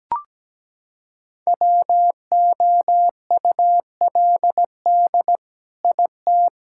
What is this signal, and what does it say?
Morse code, heard as a keyed tone, about 17 wpm, 700 Hz. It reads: WOULD IT
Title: Sentences from Top 1600 Words 17wpm